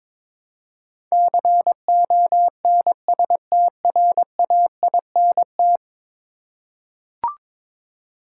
Morse code, heard as a keyed tone, about 22 wpm, 700 Hz. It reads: CONSTRAINT
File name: practice / 22wpm / 10 Letter Words - First 3 Letters Given - 22wpm